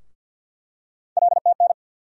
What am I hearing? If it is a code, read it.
Morse code, heard as a keyed tone, about 48 wpm, 700 Hz. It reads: LTN
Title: USA Call Sign Postfixes - 3 Letters 50wpm